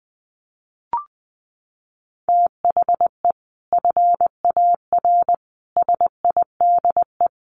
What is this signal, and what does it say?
Morse code, heard as a keyed tone, about 20 wpm, 700 Hz. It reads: THE FAR SIDE
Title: Sets of 5 Words 20wpm